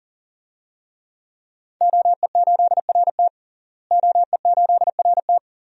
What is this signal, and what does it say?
Morse code, heard as a keyed tone, about 40 wpm, 700 Hz. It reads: OE8RT OE8RT
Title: DXCC-100 Call Signs - 5 Characters - Repeated 3x - 40wpm